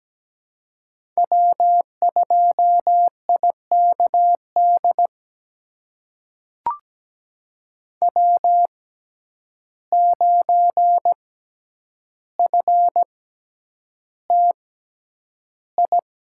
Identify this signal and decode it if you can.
Morse code, heard as a keyed tone, about 17 wpm, spaced as for 9 wpm, 700 Hz. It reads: W2IKD W9FTI
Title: USA Call Signs - 5 Characters - 6x Character Spacing - 17wpm